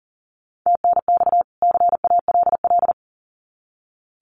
Morse code, heard as a keyed tone, about 40 wpm, 700 Hz. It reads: TNX CALL
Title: Common QSO Phrases 40wpm